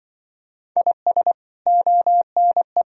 Morse code, heard as a keyed tone, about 26 wpm, 700 Hz. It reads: IS ONE